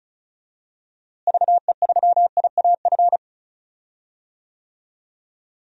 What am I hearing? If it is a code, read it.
Morse code, heard as a keyed tone, about 34 wpm, 700 Hz. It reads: VE3IAF
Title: DXCC-100 Call Signs - 6 Characters - Repeated 2x - 35wpm